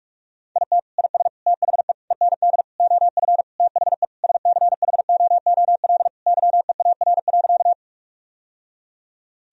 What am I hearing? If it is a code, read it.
Morse code, heard as a keyed tone, about 46 wpm, 700 Hz. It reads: IT IS THE END OF THE SCHOOL YEAR.
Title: Sentences from Top 1000 Words - Review Part 1 - 45wpm